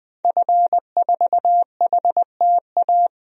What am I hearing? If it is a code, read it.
Morse code, heard as a keyed tone, about 20 wpm, 700 Hz. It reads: F4HTA
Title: DXCC-100 Call Signs - 5 Characters - 20wpm